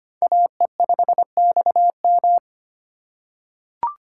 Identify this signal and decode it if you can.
Morse code, heard as a keyed tone, about 25 wpm, 700 Hz. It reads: AE5XM
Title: USA Call Signs - 2x Character Spacing - 25wpm